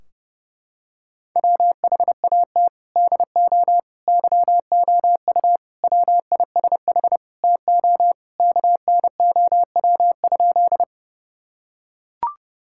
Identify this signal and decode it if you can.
Morse code, heard as a keyed tone, about 32 wpm, 700 Hz. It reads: WHAT DO YOU WISH TO KNOW?
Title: Sentences from Top 1600 Words 30wpm